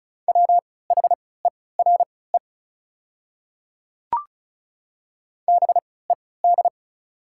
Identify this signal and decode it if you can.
Morse code, heard as a keyed tone, about 35 wpm, spaced as for 20 wpm, 690 Hz. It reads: WHERE BED
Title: 3x Character Spacing - Top 500 Words - 35wpm